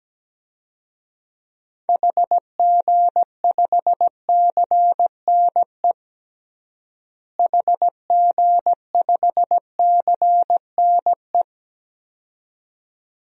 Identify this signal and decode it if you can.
Morse code, heard as a keyed tone, about 17 wpm, 700 Hz. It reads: HG5CNE HG5CNE
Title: DXCC-100 Call Signs - 6 Characters - Repeated 2x - 17wpm